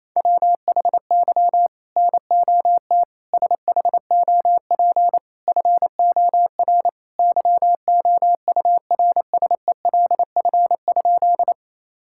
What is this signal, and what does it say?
Morse code, heard as a keyed tone, about 28 wpm, 700 Hz. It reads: WHY NOT SHOP FOR YOURSELF?
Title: Sentences from Top 1600 Words 28wpm